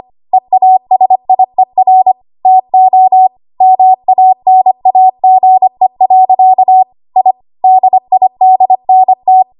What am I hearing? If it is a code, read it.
Morse code, heard as a keyed tone, about 25 wpm, 750 Hz. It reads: EASIER TO MANAGE. I DIDNT
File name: Archive / 25 WPM / 25WPM